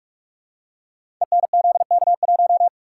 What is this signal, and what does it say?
Morse code, heard as a keyed tone, about 44 wpm, 700 Hz. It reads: ENZK1
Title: Code Groups - 5 Letter:Numbers - 45wpm